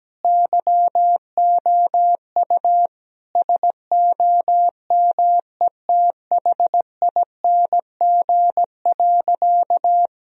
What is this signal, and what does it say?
Morse code, heard as a keyed tone, about 17 wpm, 700 Hz. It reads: YOU SOMETHING.
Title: Sentences from Top 1000 Words - Review Part 3 - 17wpm